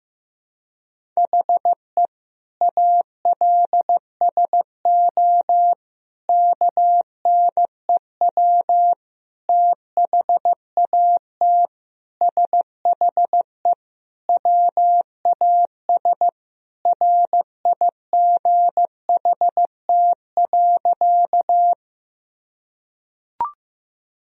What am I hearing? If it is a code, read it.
Morse code, heard as a keyed tone, about 15 wpm, 700 Hz. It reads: HE ALSO KNEW THAT SHE WAS RIGHT.
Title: Sentences from Top 1000 Words - Review Part 1 - 15wpm